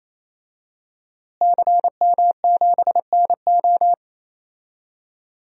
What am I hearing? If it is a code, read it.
Morse code, heard as a keyed tone, about 28 wpm, 700 Hz. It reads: CM7NO
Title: DXCC-100 Call Signs - 5 Characters - Repeated 3x - 28wpm